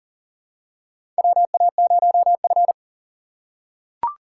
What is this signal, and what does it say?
Morse code, heard as a keyed tone, about 40 wpm, 700 Hz. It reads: WA0F
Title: USA Call Signs - 3x Character Spacing - 40wpm